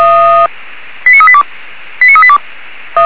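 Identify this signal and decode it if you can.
Five-tone selective call (ZVEI), heard as quick runs of five tones, tones of about 70 ms. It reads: 89382 89382
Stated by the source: ZVEI five-tone selective call, code 89376, received by SDR.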